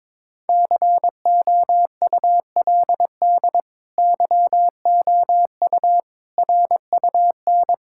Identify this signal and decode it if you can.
Morse code, heard as a keyed tone, about 22 wpm, 700 Hz. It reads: COULD YOU RUN
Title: Sentences from Top 1600 Words 22wpm